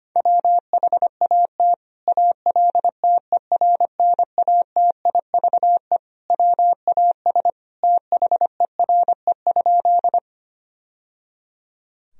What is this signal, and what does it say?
Morse code, heard as a keyed tone, about 25 wpm, 700 Hz. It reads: WHAT ALTERNATIVE WAS THERE?
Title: Sentences from Top 1000 Words - Review Part 4 - 25wpm